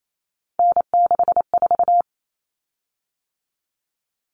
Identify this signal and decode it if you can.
Morse code, heard as a keyed tone, about 28 wpm, 700 Hz. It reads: N64
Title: Common Cut Numbers - 3-Digit 28wpm